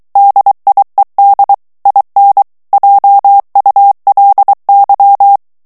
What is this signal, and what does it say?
Morse code, heard as a keyed tone, about 23 wpm, 800 Hz. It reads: DIED IN JULY
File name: morse mp3 / ANGLAIS texte clair 2 vitesse 22 mots minute